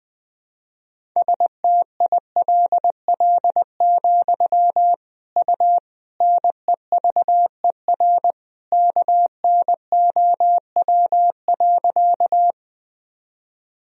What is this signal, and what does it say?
Morse code, heard as a keyed tone, about 20 wpm, 700 Hz. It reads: STILL, U NEVER KNOW.
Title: Sentences from Top 300 Words and Abbreviations 20wpm